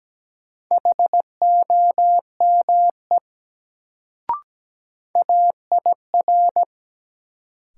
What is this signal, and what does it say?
Morse code, heard as a keyed tone, about 17 wpm, 700 Hz. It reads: HOME AIR
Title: Sentences from Top 200 Words - New Word Drill - 17wpm